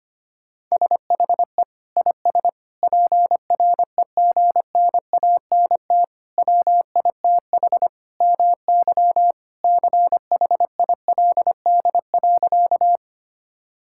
Morse code, heard as a keyed tone, about 25 wpm, 700 Hz. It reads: SHE IS PREGNANT WITH MY CHILD.
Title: Sentences from Top 1000 Words - Review Part 6 - 25wpm